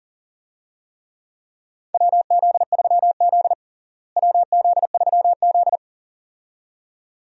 Morse code, heard as a keyed tone, about 40 wpm, 700 Hz. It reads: WZ3Z WZ3Z
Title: USA Call Signs - 4 Characters - Repeated 2x - 40wpm